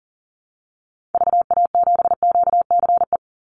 Morse code, heard as a keyed tone, about 40 wpm, 700 Hz. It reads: VA7QCE
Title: Call Signs Repeated 3x - 40wpm